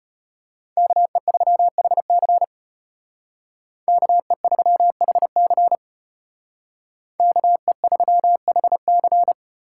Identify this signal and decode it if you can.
Morse code, heard as a keyed tone, about 33 wpm, 700 Hz. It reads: KE3HC KE3HC KE3HC